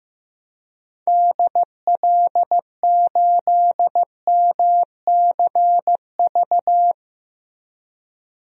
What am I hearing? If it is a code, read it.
Morse code, heard as a keyed tone, about 16 wpm, 700 Hz. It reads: DL8MCV
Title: DXCC-100 Call Signs - 6 Characters - 15wpm